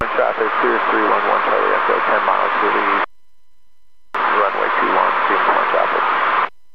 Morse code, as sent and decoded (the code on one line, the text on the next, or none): none